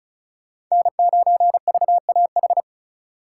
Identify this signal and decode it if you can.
Morse code, heard as a keyed tone, about 39 wpm, 700 Hz. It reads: N9VAH